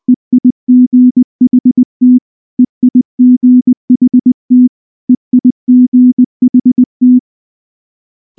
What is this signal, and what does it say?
Morse code, heard as a keyed tone, about 20 wpm, 250 Hz. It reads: EIGHT EIGHT EIGHT
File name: morse / morse-training-20WPM-2